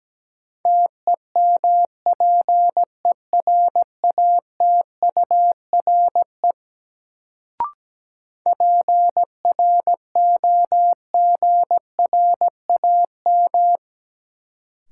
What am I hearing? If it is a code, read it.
Morse code, heard as a keyed tone, about 17 wpm, 700 Hz. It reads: TEMPERATURE PROGRAM
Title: Sentences from Top 1500 Words - New Word Drill Alt - 17wpm